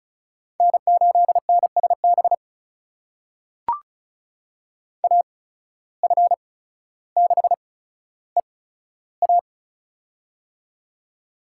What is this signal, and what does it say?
Morse code, heard as a keyed tone, about 36 wpm, spaced as for 15 wpm, 700 Hz. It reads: N8NSB AF6EA
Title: USA Call Signs - 5 Characters - 8x Character Spacing - 35wpm